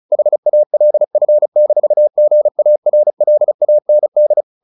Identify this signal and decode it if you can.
Morse code, heard as a keyed tone, about 34 wpm, 600 Hz. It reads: HALF-GARLAND
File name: the old man and the sea / the old man and the sea 1 35wpm